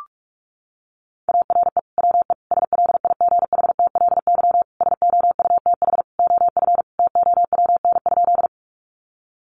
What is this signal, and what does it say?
Morse code, heard as a keyed tone, about 44 wpm, 700 Hz. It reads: ARE WE SLIGHTLY SOUTH OF TOWN?
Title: Sentences from Top 1300 Words 45wpm